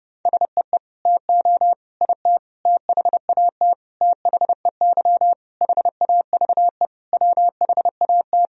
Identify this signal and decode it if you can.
Morse code, heard as a keyed tone, about 32 wpm, 700 Hz. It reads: SEE TO IT THAT THEY HAVE WHAT
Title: Sentences from Top 1000 Words - Review Part 2 - 30wpm